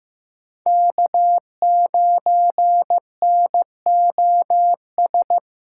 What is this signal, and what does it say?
Morse code, heard as a keyed tone, about 18 wpm, 700 Hz. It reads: K9NOS
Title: Speed Racing - USA 1x3 Call Signs 15wpm